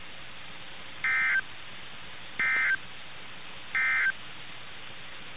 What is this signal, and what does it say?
Emergency Alert System (SAME) tones, heard as harsh buzzing bursts. It is heard from 0.9 to 4.2 s.
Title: NOAA Weather Radio - KIH61